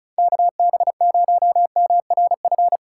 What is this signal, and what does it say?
Morse code, heard as a keyed tone, about 34 wpm, 700 Hz. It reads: KB0MRF
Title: USA Call Signs - 3x Character Spacing - 35wpm